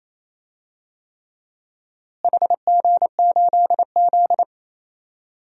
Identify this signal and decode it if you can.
Morse code, heard as a keyed tone, about 28 wpm, 700 Hz. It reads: HG8Z
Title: DXCC-100 Call Signs - 4 Characters - Repeated 3x - 28wpm